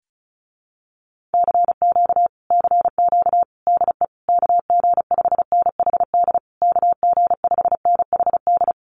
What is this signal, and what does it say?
Morse code, heard as a keyed tone, about 35 wpm, 700 Hz. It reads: CQ CQ DE KG5NHD KG5NHD
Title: Common QSO Phrases Encore 35wpm